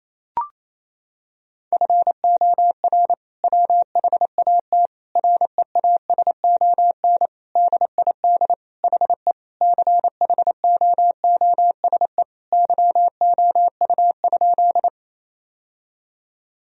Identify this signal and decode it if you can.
Morse code, heard as a keyed tone, about 28 wpm, 700 Hz. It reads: FOR WHAT REASON DID HE CHOOSE YOU?
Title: Sentences from Top 1000 Words - Review Part 4 - 28wpm